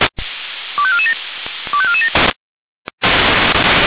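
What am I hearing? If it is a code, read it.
Five-tone selective call (ZVEI), heard as quick runs of five tones, tones of about 70 ms. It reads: 25447 25447
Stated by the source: ZVEI five-tone call 25425 received by SDR.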